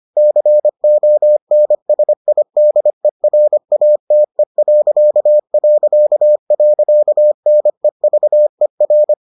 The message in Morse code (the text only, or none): CONSIDERATE...NEVER